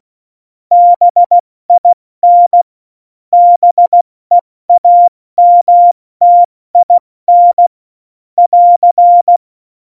Morse code, heard as a keyed tone, about 16 wpm, 700 Hz. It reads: BIN BEAMTIN <AR>
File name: Morsekurs / Witze / Witz 19. Januar 2021, 70 F80